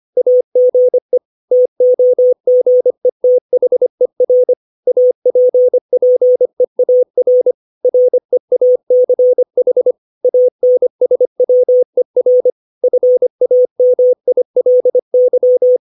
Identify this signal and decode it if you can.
Morse code, heard as a keyed tone, about 25 wpm, 500 Hz. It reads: AGE TOGETHER APPEAR REACH ANSWER FAMILY
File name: 632 words / 632 Words 25WPM